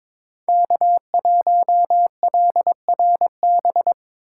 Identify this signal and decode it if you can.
Morse code, heard as a keyed tone, about 22 wpm, 700 Hz. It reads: K1LRB
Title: USA Call Signs - 5 Characters - 2x Character Spacing - 22wpm